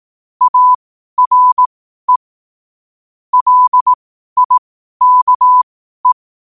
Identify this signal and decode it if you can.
Morse code, heard as a keyed tone, about 18 wpm, spaced as for 12 wpm, 1000 Hz. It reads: ARE LIKE
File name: qotdfiles / 12wpm